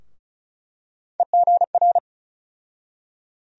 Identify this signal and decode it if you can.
Morse code, heard as a keyed tone, about 34 wpm, 700 Hz. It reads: EGR
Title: USA Call Sign Postfixes - 3 Letters 35wpm